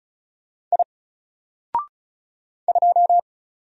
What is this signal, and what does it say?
Morse code, heard as a keyed tone, about 35 wpm, 700 Hz. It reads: I 2
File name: Single-Letter-Number-35wpm